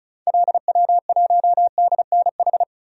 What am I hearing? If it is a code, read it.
Morse code, heard as a keyed tone, about 34 wpm, 700 Hz. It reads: LW1DNH
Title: DXCC-100 Call Signs - 6 Characters - Repeated 2x - 35wpm